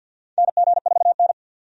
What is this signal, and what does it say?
Morse code, heard as a keyed tone, about 49 wpm, 700 Hz. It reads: NG4N
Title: USA Call Signs - 4 Characters - 50wpm